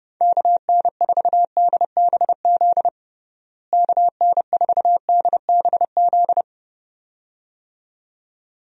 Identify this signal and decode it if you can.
Morse code, heard as a keyed tone, about 30 wpm, 700 Hz. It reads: KN4DBZ KN4DBZ
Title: USA Call Signs - 6 Characters - Repeated 2x - 30wpm